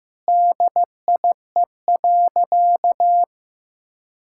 Morse code, heard as a keyed tone, about 15 wpm, 700 Hz. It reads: DIE.
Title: Sentences from Top 1600 Words 15wpm